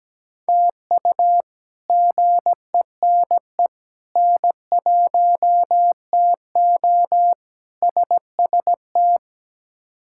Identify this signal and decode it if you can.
Morse code, heard as a keyed tone, about 17 wpm, 700 Hz. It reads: TU GENE N1TO SST
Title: K1USN SST QSO Practice - 17wpm